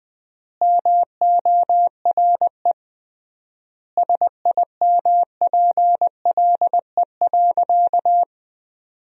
Morse code, heard as a keyed tone, about 20 wpm, 700 Hz. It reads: MORE SIMPLE.
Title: 3x Word-Spacing - Sentences from Top 400 Words - 20wpm